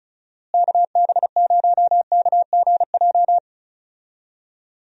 Morse code, heard as a keyed tone, about 34 wpm, 700 Hz. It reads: KB0KGJ
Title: USA Call Signs - 6 Characters - Repeated 3x - 35wpm